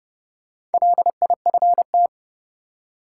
Morse code, heard as a keyed tone, about 30 wpm, 700 Hz. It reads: LIFT
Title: Four Letter Words - 30wpm